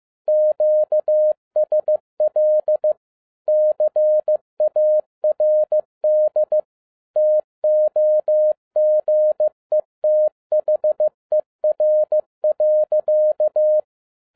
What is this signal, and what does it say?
Morse code, heard as a keyed tone, about 15 wpm, 610 Hz. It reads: QSL CARD TOGETHER.
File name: 2012-01-mp3 / n13-A NEWLY DESIGNED QSL CARD-15wpm